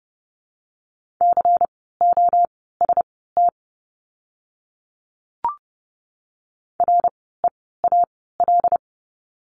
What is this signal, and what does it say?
Morse code, heard as a keyed tone, about 30 wpm, spaced as for 17 wpm, 700 Hz. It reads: COST REAL
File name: Lesson 16f - Letter c - Words 30wpm